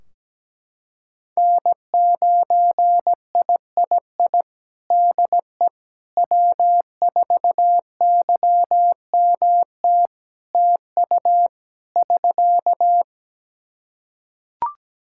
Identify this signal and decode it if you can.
Morse code, heard as a keyed tone, about 17 wpm, 700 Hz. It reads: N9III DE W4YMT TU <SK>
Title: Practice QSO - Exchange 4 - Ending - 17wpm